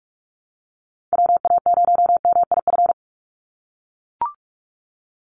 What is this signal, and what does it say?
Morse code, heard as a keyed tone, about 44 wpm, 700 Hz. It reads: WA0MIF
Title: USA Call Signs 45wpm